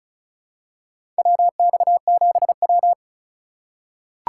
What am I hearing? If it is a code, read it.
Morse code, heard as a keyed tone, about 34 wpm, 700 Hz. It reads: WX7W
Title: USA Call Signs - 4 Characters - 6x Character Spacing - 35wpm